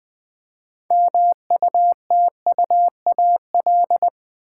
Morse code, heard as a keyed tone, about 20 wpm, 700 Hz. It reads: MUTUAL
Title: 6 Letter Words - First 3 Letters Given - 20wpm